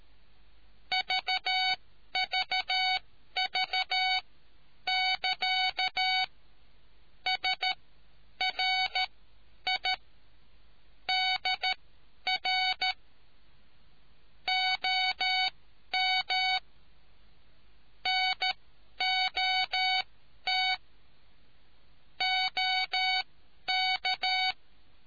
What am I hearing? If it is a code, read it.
Morse code, heard as a keyed tone, about 13 wpm, spaced as for 10 wpm, 750 Hz. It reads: VVV<KA> SRI DR OM NOT OK